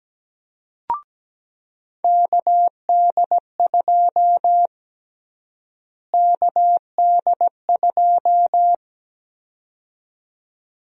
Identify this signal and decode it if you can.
Morse code, heard as a keyed tone, about 17 wpm, 700 Hz. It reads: KD2 KD2
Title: USA Call Sign Prefixes - Repeated 2x - 17wpm